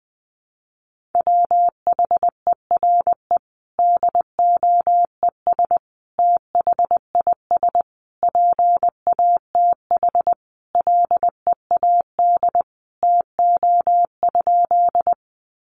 Morse code, heard as a keyed tone, about 20 wpm, 700 Hz. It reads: WHERE DOES THIS PATH LEAD TO?
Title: Sentences from Top 1400 Words 20wpm